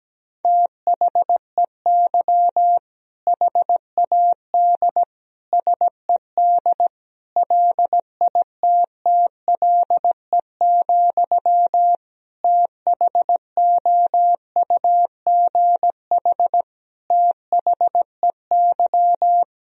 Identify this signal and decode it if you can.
Morse code, heard as a keyed tone, about 17 wpm, 700 Hz. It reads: THEY HAD SED LITTLE, THOUGH THEY
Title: Sentences from Top 400 Words and Abbreviations 17wpm